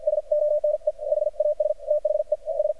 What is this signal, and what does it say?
Morse code, heard as a keyed tone, about 53 wpm, spaced as for 46 wpm, 600 Hz. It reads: S ONE HAS THE S
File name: archived / CRSnet